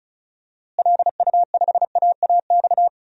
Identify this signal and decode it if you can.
Morse code, heard as a keyed tone, about 36 wpm, 700 Hz. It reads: LU5AAX
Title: DXCC-100 Call Signs - 6 Characters - Repeated 3x - 35wpm